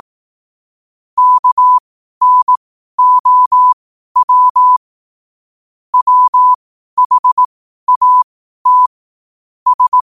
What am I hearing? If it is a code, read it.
Morse code, heard as a keyed tone, about 18 wpm, spaced as for 12 wpm, 1000 Hz. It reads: KNOW WHAT S
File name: qotdfiles / 12wpm